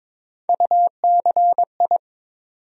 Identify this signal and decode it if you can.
Morse code, heard as a keyed tone, about 22 wpm, 700 Hz. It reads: UCI
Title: USA Call Sign Postfixes - 3 Letters 22wpm